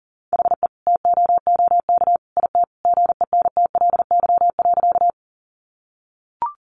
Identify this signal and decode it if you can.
Morse code, heard as a keyed tone, about 40 wpm, 700 Hz. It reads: HE TOOK IT GENTLY.
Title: Sentences from Top 1500 Words 40wpm